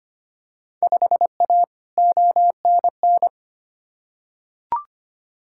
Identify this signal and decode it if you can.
Morse code, heard as a keyed tone, about 25 wpm, 700 Hz. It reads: 5A ONN
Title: Field Day Exchanges Encore - 25wpm